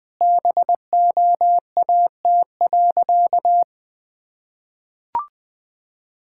Morse code, heard as a keyed tone, about 20 wpm, 700 Hz. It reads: BOAT.
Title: Sentences from Top 1000 Words - Review Part 5 - 20wpm